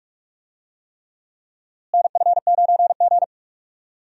Morse code, heard as a keyed tone, about 45 wpm, 700 Hz. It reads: NF9G
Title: USA Call Signs - 4 Characters - Repeated 3x - 45wpm